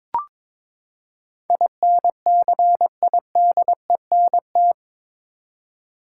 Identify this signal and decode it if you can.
Morse code, heard as a keyed tone, about 22 wpm, 700 Hz. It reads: INCIDENT
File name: practice / 22wpm / Eight Letter Words - 22wpm